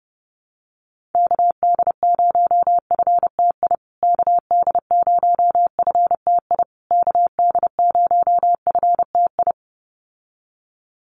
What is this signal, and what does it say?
Morse code, heard as a keyed tone, about 30 wpm, 700 Hz. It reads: KD0FTI KD0FTI KD0FTI
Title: USA Call Signs Repeated 3x 30wpm